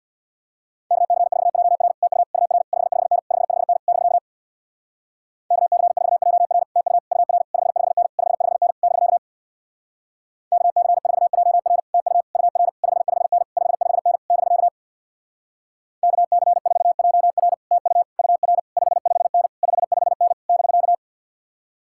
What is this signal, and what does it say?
Morse code, heard as a keyed tone, about 64 wpm, 700 Hz. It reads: KC4JR TU UR 55N 55N <BK> KC4JR TU UR 55N 55N <BK> KC4JR TU UR 55N 55N <BK> KC4JR TU UR 55N 55N <BK>